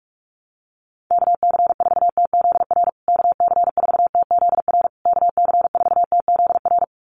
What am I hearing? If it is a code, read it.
Morse code, heard as a keyed tone, about 44 wpm, 700 Hz. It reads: KC4TZR KC4TZR KC4TZR
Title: USA Call Signs Repeated 3x 45wpm